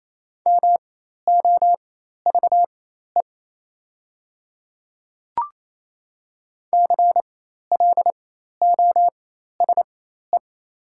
Morse code, heard as a keyed tone, about 27 wpm, spaced as for 13 wpm, 700 Hz. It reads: MOVE CLOSE
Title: 4x Character Spacing - Top 500 Words - 28wpm